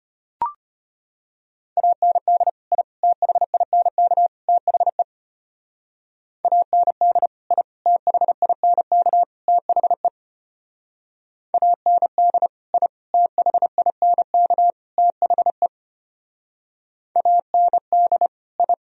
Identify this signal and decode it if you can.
Morse code, heard as a keyed tone, about 33 wpm, 700 Hz. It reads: AND I THINK THE AND I THINK THE AND I THINK THE AND I